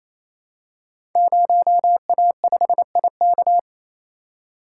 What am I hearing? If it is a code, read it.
Morse code, heard as a keyed tone, about 27 wpm, 700 Hz. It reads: 0A5IK
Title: Code Groups - 5 Letter:Numbers - 28wpm